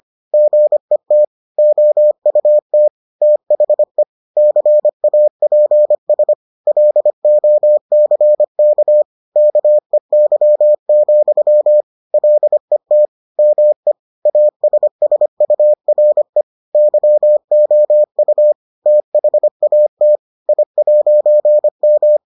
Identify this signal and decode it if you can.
Morse code, heard as a keyed tone, about 25 wpm, 600 Hz. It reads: GET OUT THE CAPS LOCK KEY, LET ME ASSURE YOU THAT I'M